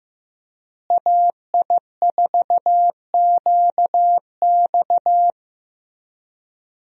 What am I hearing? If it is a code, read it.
Morse code, heard as a keyed tone, about 15 wpm, 700 Hz. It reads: AI4QX
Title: USA Call Signs - 5 Characters - Repeated 2x - 15wpm